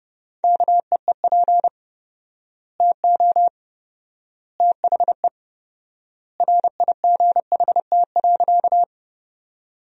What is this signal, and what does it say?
Morse code, heard as a keyed tone, about 32 wpm, 700 Hz. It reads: KEEP TO THE RIGHT.
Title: 4x Word-Spacing - Sentences from Top 300 Words - 30wpm